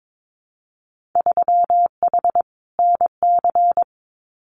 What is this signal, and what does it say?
Morse code, heard as a keyed tone, about 22 wpm, 700 Hz. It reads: 3H NC
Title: Winter Field Day Exchanges 22wpm